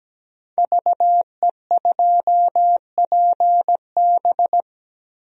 Morse code, heard as a keyed tone, about 17 wpm, 700 Hz. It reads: VE2PB
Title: DXCC-100 Call Signs - 5 Characters - 17wpm